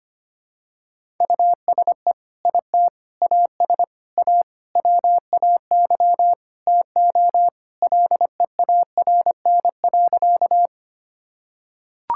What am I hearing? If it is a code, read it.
Morse code, heard as a keyed tone, about 25 wpm, 700 Hz. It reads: USE IT AS A WAY TO LEARN.
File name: practice / 25wpm / Sentences with 4-Word Phrases - 25wpm